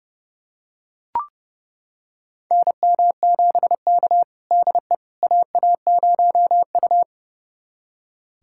Morse code, heard as a keyed tone, about 30 wpm, 700 Hz. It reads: NM7K DE AA0U
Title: USA Callsign DE Callsign - 4 Characters - 30wpm